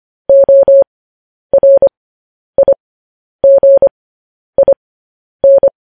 Morse code, heard as a keyed tone, about 25 wpm, spaced as for 10 wpm, 550 Hz. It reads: ORIGIN
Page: news headlines in morse code at 10 wpm